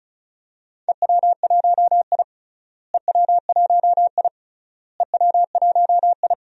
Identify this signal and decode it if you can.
Morse code, heard as a keyed tone, about 36 wpm, 700 Hz. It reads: EW1I EW1I EW1I
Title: DXCC-100 Call Signs - 4 Characters - Repeated 3x - 35wpm